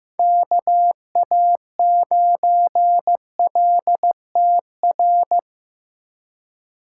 Morse code, heard as a keyed tone, about 15 wpm, 700 Hz. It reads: KA9LTR
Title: USA Call Signs - 6 Characters - 4x Character Spacing - 15wpm